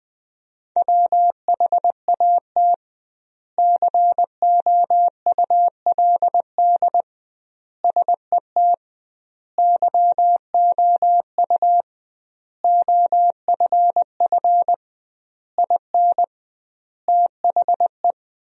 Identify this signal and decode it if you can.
Morse code, heard as a keyed tone, about 20 wpm, 700 Hz. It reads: WHAT COULD SET YOU OFF IN THE
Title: 2x Word-Spacing - Sentences from Top 200 Words - 20wpm